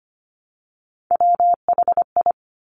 Morse code, heard as a keyed tone, about 25 wpm, 700 Hz. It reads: WHI
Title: Top Three Letter Combinations in Words 25wpm